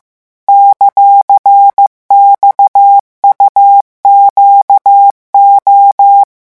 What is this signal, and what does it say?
Morse code, heard as a keyed tone, about 15 wpm, 800 Hz. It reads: ;XUQO